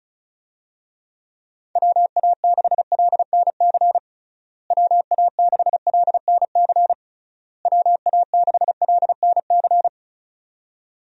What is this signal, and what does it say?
Morse code, heard as a keyed tone, about 34 wpm, 700 Hz. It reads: WA6LNC WA6LNC WA6LNC
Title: USA Call Signs - 6 Characters - Repeated 3x - 35wpm